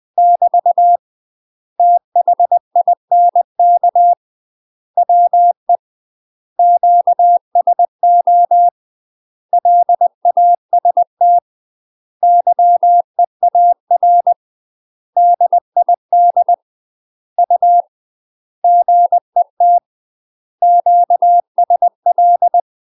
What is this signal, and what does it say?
Morse code, heard as a keyed tone, about 20 wpm, 700 Hz. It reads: <BT> THINK WE QSO LAST YEAR DID U GET QSL
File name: UK MASTER QSOs TRAINING / 20WPM / UK MASTER QSOs 0013